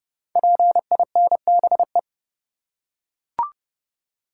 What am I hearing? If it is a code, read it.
Morse code, heard as a keyed tone, about 30 wpm, 700 Hz. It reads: PINBE
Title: Code Groups - 5 Letter:Numbers - 30wpm